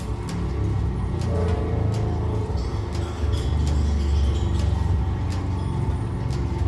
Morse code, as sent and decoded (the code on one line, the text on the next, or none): none